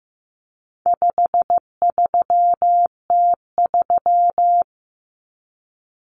Morse code, heard as a keyed tone, about 15 wpm, 700 Hz. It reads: 53T3
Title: Common Cut Numbers - 4-Digit 15wpm